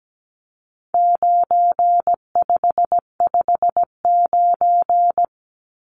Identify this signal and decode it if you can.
Morse code, heard as a keyed tone, about 17 wpm, 700 Hz. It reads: 9559
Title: Numbers - 4-Digit 17wpm